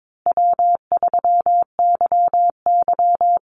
Morse code, heard as a keyed tone, about 22 wpm, 700 Hz. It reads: W3YY
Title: Easy Call Signs Repeated 3x - 22wpm